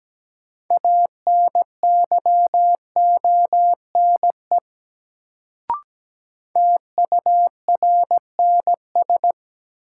Morse code, heard as a keyed tone, about 17 wpm, 700 Hz. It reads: ANYONE TURNS
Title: Sets of 5 Words Encore - New Word Drill - 17wpm